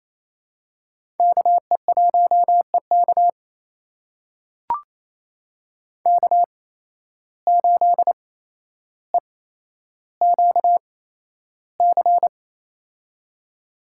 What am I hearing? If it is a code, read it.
Morse code, heard as a keyed tone, about 28 wpm, spaced as for 12 wpm, 700 Hz. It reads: KE1EK K8EQC
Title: USA Call Signs - 5 Characters - 8x Character Spacing - 28wpm